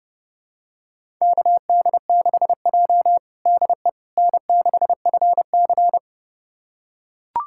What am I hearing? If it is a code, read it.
Morse code, heard as a keyed tone, about 32 wpm, 700 Hz. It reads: KD6J DE N6FC
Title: USA Callsign DE Callsign - 4 Characters - 30wpm